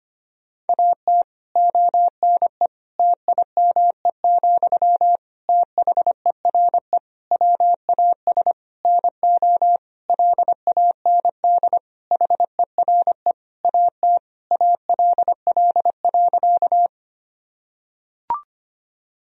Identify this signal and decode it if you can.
Morse code, heard as a keyed tone, about 25 wpm, 700 Hz. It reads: AT ONE TIME, THERE WAS NO LAND HERE AT ALL.
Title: Sentences from Top 1000 Words - Review Part 1 - 25wpm